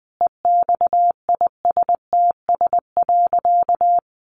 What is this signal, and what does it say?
Morse code, heard as a keyed tone, about 20 wpm, 700 Hz. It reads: EXISTS.
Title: A Sound Of Thunder 20wpm